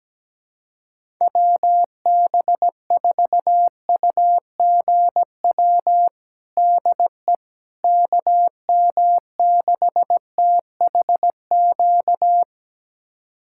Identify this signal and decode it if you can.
Morse code, heard as a keyed tone, about 17 wpm, 700 Hz. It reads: WB4UGW DE KM6THQ
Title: USA Callsign DE Callsign - 6 Characters - 17wpm